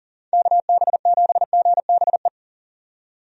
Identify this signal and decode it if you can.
Morse code, heard as a keyed tone, about 40 wpm, 700 Hz. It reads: KB7GBE
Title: USA Call Signs - 6 Characters - 40wpm